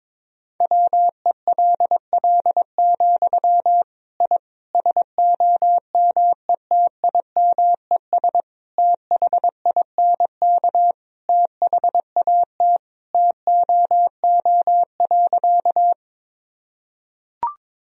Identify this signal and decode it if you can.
Morse code, heard as a keyed tone, about 22 wpm, 700 Hz. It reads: WELL, I SOMETIMES THINK THAT TOO.
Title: Sentences from Top 1000 Words - Review Part 3 - 22wpm